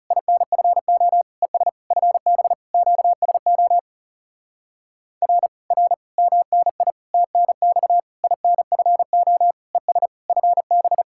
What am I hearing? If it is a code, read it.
Morse code, heard as a keyed tone, about 40 wpm, 700 Hz. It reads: INFO ES FB QSO R R MNI TNX INFO ES FB